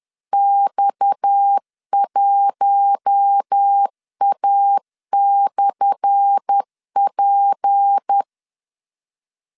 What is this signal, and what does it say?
Morse code, heard as a keyed tone, about 10 wpm, 800 Hz. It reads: X1A/P
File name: LEZIONI / 10 SIGNS / 10B INT 10wpm 2844